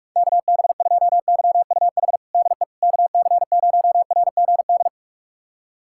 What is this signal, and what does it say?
Morse code, heard as a keyed tone, about 46 wpm, 690 Hz. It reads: KB2YUH DE KC0RGD
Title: USA Callsign DE Callsign - 6 Characters - 45wpm